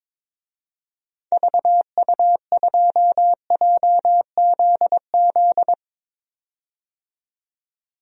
Morse code, heard as a keyed tone, about 22 wpm, 700 Hz. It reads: VU2JZZ
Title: DXCC-100 Call Signs - 6 Characters - 22wpm